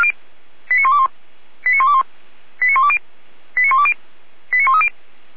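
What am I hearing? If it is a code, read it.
Five-tone selective call (ZVEI), heard as quick runs of five tones, tones of about 70 ms. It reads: 89121 89121 89120 89120 89130
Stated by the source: A ZVEI five-tone selective call received by SDR.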